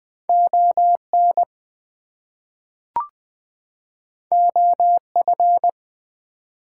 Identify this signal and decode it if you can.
Morse code, heard as a keyed tone, about 20 wpm, 700 Hz. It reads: ON OF
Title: Two Letter Words - 20wpm